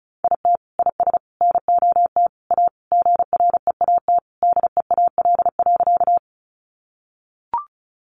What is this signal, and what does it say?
Morse code, heard as a keyed tone, about 34 wpm, 700 Hz. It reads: IT IS NOT A GREAT DEAL.
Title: Sentences from Top 800 Words 35wpm